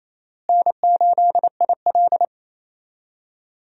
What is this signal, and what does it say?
Morse code, heard as a keyed tone, about 28 wpm, 700 Hz. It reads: N8IL
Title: USA Call Signs - 4 Characters - Repeated 2x - 28wpm